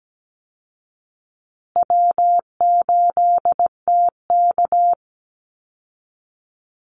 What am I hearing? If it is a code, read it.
Morse code, heard as a keyed tone, about 17 wpm, 700 Hz. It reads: W8TK
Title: Easy Call Signs Repeated 3x - 17wpm